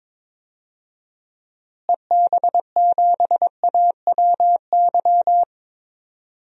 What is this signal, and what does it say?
Morse code, heard as a keyed tone, about 22 wpm, 700 Hz. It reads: EB7AWY
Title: DXCC-100 Call Signs - 6 Characters - Repeated 2x - 22wpm